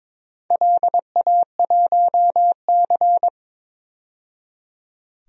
Morse code, heard as a keyed tone, about 22 wpm, 700 Hz. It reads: LA1C
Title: DXCC-100 Call Signs - 4 Characters - 22wpm